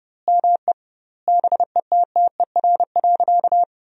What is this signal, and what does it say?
Morse code, heard as a keyed tone, about 30 wpm, 700 Hz. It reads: ME BETTER.
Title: 2x Word-Spacing - Sentences from Top 400 Words - 30wpm